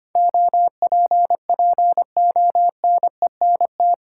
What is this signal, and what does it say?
Morse code, heard as a keyed tone, about 25 wpm, 690 Hz. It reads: OPPONENT
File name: 8 Letter Words - First 3 Letters Given - 25wpm